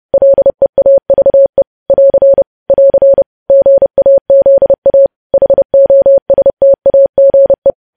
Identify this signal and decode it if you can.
Morse code, heard as a keyed tone, about 30 wpm, 550 Hz. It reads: LEAVE <AR> <AR> GAZA HOSTAGE
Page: News Headlines in Morse Code at 30 WPM